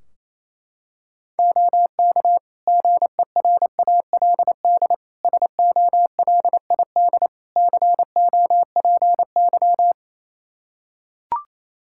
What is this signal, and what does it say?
Morse code, heard as a keyed tone, about 28 wpm, 700 Hz. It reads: OK GERALD SOLID COPY
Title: Practice QSO - Exchange 3 - More Info - 28wpm